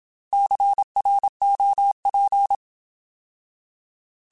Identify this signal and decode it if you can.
Morse code, heard as a keyed tone, about 27 wpm, 800 Hz. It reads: CROP
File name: Alphabetical / Alphabetical mp3s / Cs @25